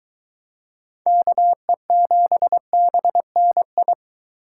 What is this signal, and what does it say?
Morse code, heard as a keyed tone, about 22 wpm, 700 Hz. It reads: KE7BNI